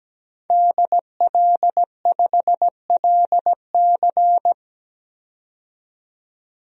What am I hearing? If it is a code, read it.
Morse code, heard as a keyed tone, about 17 wpm, 700 Hz. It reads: DL5LC
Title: DXCC-100 Call Signs - 5 Characters - 17wpm